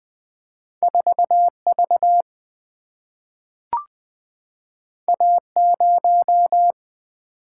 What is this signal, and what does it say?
Morse code, heard as a keyed tone, about 20 wpm, 700 Hz. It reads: 4V A0
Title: Single-Letter Single-Number 20wpm